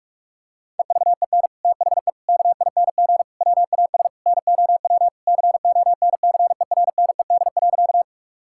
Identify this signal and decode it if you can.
Morse code, heard as a keyed tone, about 41 wpm, 700 Hz. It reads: EVEN THE KING WAS NOW CONCERNED.
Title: Sentences from Top 1000 Words - Review Part 6 - 45wpm